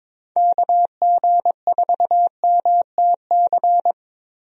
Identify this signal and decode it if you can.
Morse code, heard as a keyed tone, about 21 wpm, 700 Hz. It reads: KG4MTC
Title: USA Call Signs - 6 Characters - Repeated 3x - 22wpm